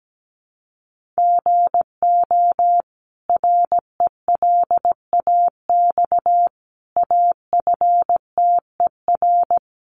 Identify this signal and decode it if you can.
Morse code, heard as a keyed tone, about 17 wpm, 700 Hz. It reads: GO RELAX AFTER
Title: Sentences from Top 1400 Words 17wpm